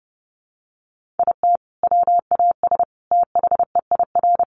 Morse code, heard as a keyed tone, about 30 wpm, 700 Hz. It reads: IT WAS THEIR
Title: Sentences from Top 1500 Words 30wpm